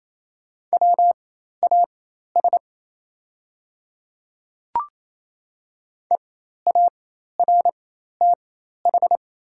Morse code, heard as a keyed tone, about 28 wpm, spaced as for 13 wpm, 700 Hz. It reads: WAS EARTH
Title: 4x Character Spacing - Top 300 Words - 28wpm